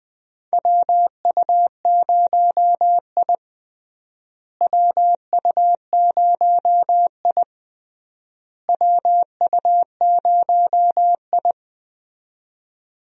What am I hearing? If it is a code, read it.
Morse code, heard as a keyed tone, about 20 wpm, 700 Hz. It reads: WU0I WU0I WU0I
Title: USA Call Signs - 4 Characters - Repeated 3x - 20wpm